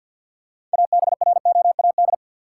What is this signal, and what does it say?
Morse code, heard as a keyed tone, about 48 wpm, 700 Hz. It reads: ABROAD
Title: Six Letter Words - 50wpm